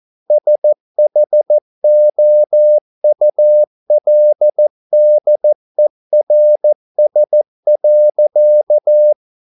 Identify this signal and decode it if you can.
Morse code, heard as a keyed tone, about 14 wpm, 600 Hz. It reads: SHOULDERS.